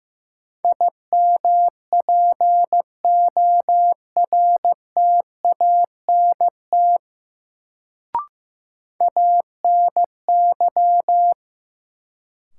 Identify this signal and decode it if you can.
Morse code, heard as a keyed tone, about 15 wpm, 700 Hz. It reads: IMPORTANT ANY
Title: Sets of 3 Words - New Word Drill - 15wpm